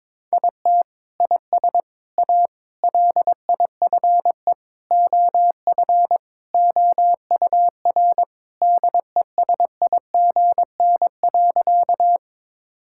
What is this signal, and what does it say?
Morse code, heard as a keyed tone, about 22 wpm, 700 Hz. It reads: IT IS A LIFE OF OUR DESIGN.
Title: Sentences from Top 1000 Words - Review Part 5 - 22wpm